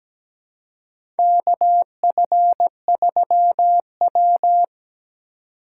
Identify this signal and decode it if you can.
Morse code, heard as a keyed tone, about 17 wpm, 700 Hz. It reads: KF3W
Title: USA Call Signs - 4 Characters - 3x Character Spacing - 17wpm